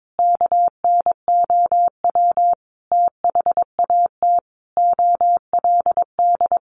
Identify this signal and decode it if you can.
Morse code, heard as a keyed tone, about 22 wpm, 700 Hz. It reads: KNOW THAT OLD
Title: Sentences from Top 400 Words 22wpm